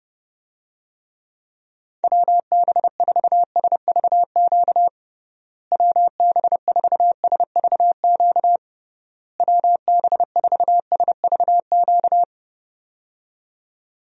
Morse code, heard as a keyed tone, about 30 wpm, 700 Hz. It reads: WB4SVQ WB4SVQ WB4SVQ
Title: USA Call Signs - 6 Characters - Repeated 3x - 30wpm